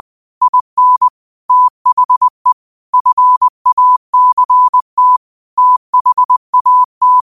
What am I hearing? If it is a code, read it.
Morse code, heard as a keyed tone, about 20 wpm, 1000 Hz. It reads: IN THE FACT THAT